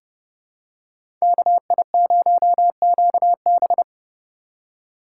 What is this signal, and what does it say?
Morse code, heard as a keyed tone, about 30 wpm, 700 Hz. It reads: KI0QB
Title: USA Call Signs - 5 Characters - 3x Character Spacing - 30wpm